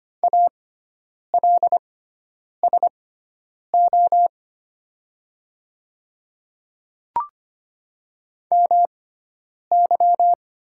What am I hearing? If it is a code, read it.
Morse code, heard as a keyed tone, about 25 wpm, spaced as for 9 wpm, 700 Hz. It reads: ALSO MY
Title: 6x Character Spacing - Top 500 Words - 25wpm